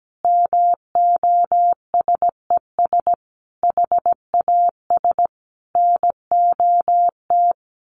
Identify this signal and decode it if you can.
Morse code, heard as a keyed tone, about 17 wpm, 700 Hz. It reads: MOSES HAS NOT
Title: A Sound Of Thunder 17wpm